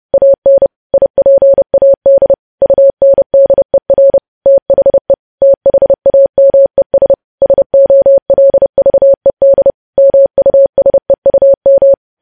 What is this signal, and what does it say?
Morse code, heard as a keyed tone, about 30 wpm, 550 Hz. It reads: AN IPAD UNDER THE THAMES SOLVED MUSEUM
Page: news headlines in morse code at 30 wpm